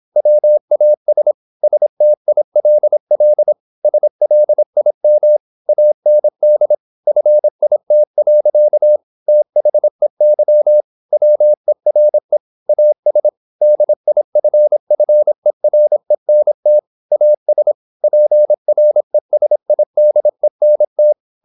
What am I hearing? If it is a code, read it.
Morse code, heard as a keyed tone, about 26 wpm, 600 Hz. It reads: WAS STILL SLIM AND FIT. THEY WERE AS DIFFERENT AS PRESIDENT
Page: Never 26wpm | CW med Gnister